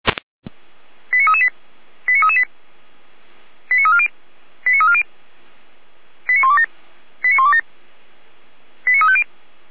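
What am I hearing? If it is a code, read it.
Five-tone selective call (ZVEI), heard as quick runs of five tones, tones of about 80 ms. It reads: 89338 89338 89340 89340 89127 89127 89350
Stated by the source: ZVEI five-tone call 89315 received by SDR.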